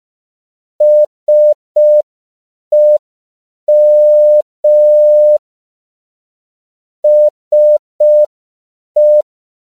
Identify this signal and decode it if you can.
Morse code, heard as a keyed tone, about 5 wpm, 600 Hz. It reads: SEM SE
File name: CW / 5PPM